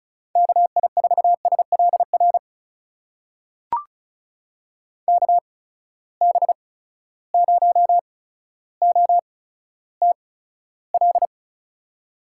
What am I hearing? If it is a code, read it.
Morse code, heard as a keyed tone, about 36 wpm, spaced as for 15 wpm, 700 Hz. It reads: KI4SLR KB0OTL
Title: USA Call Signs - 6 Characters - 8x Character Spacing - 35wpm